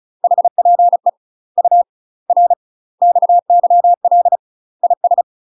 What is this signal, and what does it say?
Morse code, heard as a keyed tone, about 36 wpm, 700 Hz. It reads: HPE U R XYL IS
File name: UK MASTER QSOs TRAINING / 35WPM / UK MASTER QSOs 0035